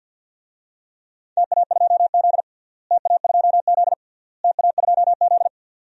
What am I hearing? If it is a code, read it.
Morse code, heard as a keyed tone, about 47 wpm, 700 Hz. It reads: TA2Z TA2Z TA2Z
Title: DXCC-100 Call Signs - 4 Characters - Repeated 3x - 50wpm